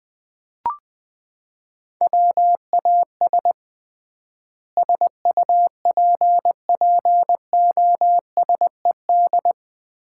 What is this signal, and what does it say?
Morse code, heard as a keyed tone, about 20 wpm, 700 Hz. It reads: WAS SUPPOSED
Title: 3x Word-Spacing - Sets of 4 Words - 20wpm